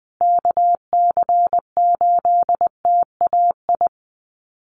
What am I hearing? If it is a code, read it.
Morse code, heard as a keyed tone, about 20 wpm, 700 Hz. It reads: KC8TAI
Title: USA Call Signs Repeated 3x 20wpm